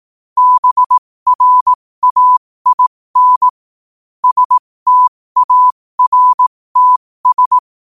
Morse code, heard as a keyed tone, about 18 wpm, spaced as for 15 wpm, 1000 Hz. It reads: BRAIN STARTS
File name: qotdfiles / 15wpm